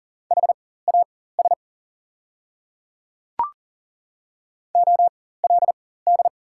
Morse code, heard as a keyed tone, about 40 wpm, spaced as for 19 wpm, 700 Hz. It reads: HAS OLD
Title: 4x Character Spacing - Top 300 Words - 40wpm